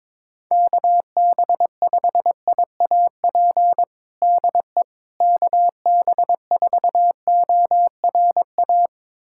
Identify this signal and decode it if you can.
Morse code, heard as a keyed tone, about 22 wpm, 700 Hz. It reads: KB5IAP DE KB4ORA
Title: USA Callsign DE Callsign - 6 Characters - 22wpm